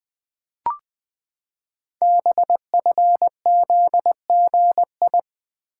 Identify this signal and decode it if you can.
Morse code, heard as a keyed tone, about 20 wpm, 700 Hz. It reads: BFZGI
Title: Code Groups - 5 Letters - 20wpm